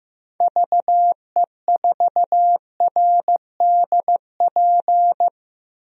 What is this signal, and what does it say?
Morse code, heard as a keyed tone, about 16 wpm, 700 Hz. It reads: VE4RDP
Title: DXCC-100 Call Signs - 6 Characters - 15wpm